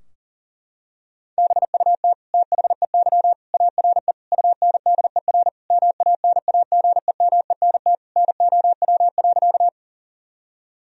Morse code, heard as a keyed tone, about 40 wpm, 700 Hz. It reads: BUT THEY ARE UNDER MANAGEMENT NOW.
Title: Sentences from Top 1000 Words - Review Part 4 - 40wpm